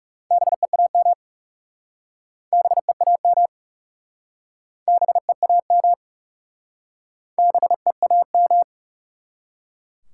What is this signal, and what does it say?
Morse code, heard as a keyed tone, about 37 wpm, 700 Hz. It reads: BEAM BEAM BEAM BEAM
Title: Speed-Racing Top 90 QSO Elements 30wpm